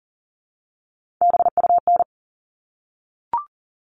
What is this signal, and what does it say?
Morse code, heard as a keyed tone, about 40 wpm, 700 Hz. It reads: BUN
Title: All Cut Numbers - 3-Digit 40wpm